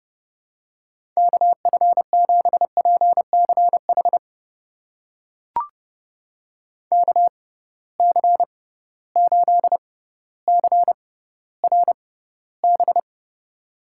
Morse code, heard as a keyed tone, about 32 wpm, spaced as for 16 wpm, 700 Hz. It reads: KF7PCH KC8CRB
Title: USA Call Signs - 6 Characters - 6x Character Spacing - 30wpm